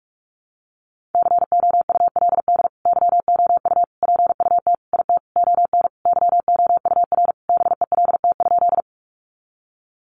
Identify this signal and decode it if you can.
Morse code, heard as a keyed tone, about 44 wpm, 700 Hz. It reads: COULD YOU PUT IT ON YOUR BELT?
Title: Sentences from Top 1400 Words 45wpm